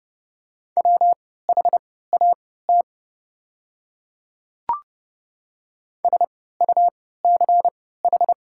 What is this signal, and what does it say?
Morse code, heard as a keyed tone, about 30 wpm, spaced as for 17 wpm, 700 Hz. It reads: WHAT SUCH
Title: 3x Character Spacing - Top 200 Words - 30wpm